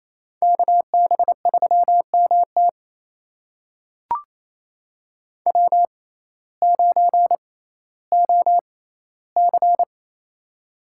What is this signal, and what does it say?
Morse code, heard as a keyed tone, about 28 wpm, spaced as for 15 wpm, 700 Hz. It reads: KB3MT W9OC
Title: USA Call Signs - 6x Character Spacing - 28wpm